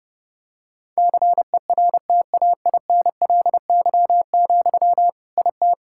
Morse code, heard as a keyed tone, about 32 wpm, 700 Hz. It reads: CERTAINLY, IT
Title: Sentences from Top 1000 Words - Review Part 6 - 30wpm